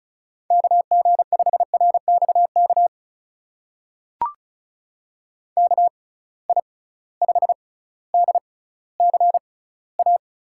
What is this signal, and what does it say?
Morse code, heard as a keyed tone, about 36 wpm, spaced as for 18 wpm, 700 Hz. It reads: KG5RXK KI5DCA
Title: USA Call Signs - 6 Characters - 6x Character Spacing - 35wpm